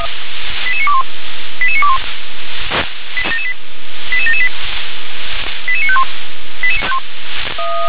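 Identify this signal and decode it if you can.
Five-tone selective call (ZVEI), heard as quick runs of five tones, tones of about 70 ms. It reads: 88921 88921 88778 88951
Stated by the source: ZVEI five-tone selective call received by SDR.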